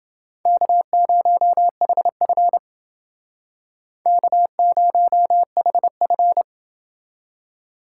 Morse code, heard as a keyed tone, about 29 wpm, 700 Hz. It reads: K0HF K0HF